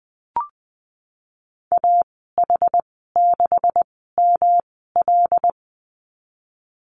Morse code, heard as a keyed tone, about 20 wpm, spaced as for 14 wpm, 700 Hz. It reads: AH6ML
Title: 128 Lesson 21g - Number 6 - Call Signs 20wpm